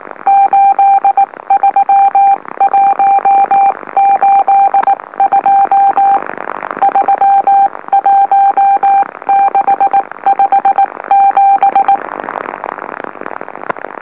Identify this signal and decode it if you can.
Morse code, heard as a keyed tone, about 19 wpm, 790 Hz. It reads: 83182 31657
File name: M04- -2345utc-LOLO